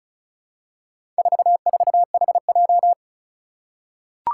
Morse code, heard as a keyed tone, about 36 wpm, 700 Hz. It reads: 44HJ
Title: Four Challenging Letter-Number Combinations 35wpm